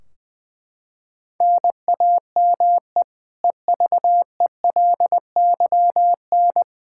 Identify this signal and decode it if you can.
Morse code, heard as a keyed tone, about 20 wpm, 700 Hz. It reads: NAME EVELYN
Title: Practice QSO - Exchange 1 - Beginning - 20wpm